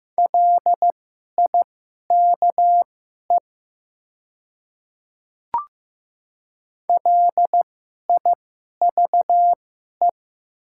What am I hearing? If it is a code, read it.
Morse code, heard as a keyed tone, about 15 wpm, spaced as for 11 wpm, 700 Hz. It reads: LIKE LIVE
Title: 2x Character Spacing - Top 300 Words - 15wpm